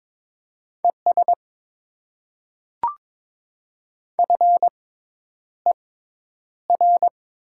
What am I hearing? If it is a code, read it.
Morse code, heard as a keyed tone, about 21 wpm, spaced as for 10 wpm, 700 Hz. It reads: ES FER
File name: Top 20 QSO Elements - 6x Character Spacing - 22wpm